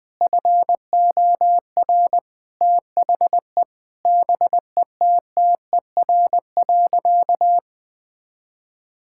Morse code, heard as a keyed tone, about 20 wpm, 700 Hz. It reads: FOR THE BETTER.
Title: Sentences from Top 1000 Words - Review Part 5 - 20wpm